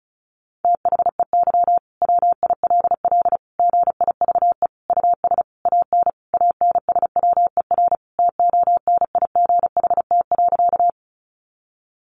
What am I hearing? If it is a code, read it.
Morse code, heard as a keyed tone, about 34 wpm, 700 Hz. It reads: THEY WILL GIVE US AN ANSWER TONIGHT.
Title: Sentences from Top 600 Words 35wpm